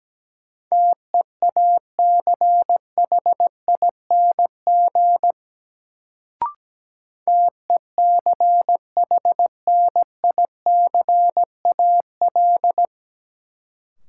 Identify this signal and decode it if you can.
Morse code, heard as a keyed tone, about 17 wpm, 700 Hz. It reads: TEACHING TECHNICAL
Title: Sentences from Top 1100 Words - New Word Drill - 17wpm